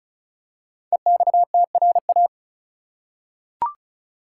Morse code, heard as a keyed tone, about 36 wpm, 700 Hz. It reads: EXTRA
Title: Five Letter Words - 35wpm